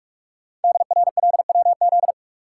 Morse code, heard as a keyed tone, about 45 wpm, 700 Hz. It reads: DRLWZ
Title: Code Groups - 5 Letters - 45wpm